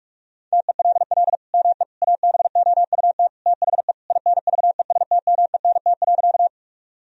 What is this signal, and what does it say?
Morse code, heard as a keyed tone, about 46 wpm, 700 Hz. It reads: TELL ME ABOUT THE INVESTMENT.
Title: Sentences from Top 1000 Words - Review Part 4 - 45wpm